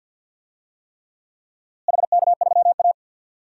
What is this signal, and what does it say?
Morse code, heard as a keyed tone, about 52 wpm, 700 Hz. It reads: HK3A
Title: DXCC-100 Call Signs - 4 Characters - Repeated 3x - 50wpm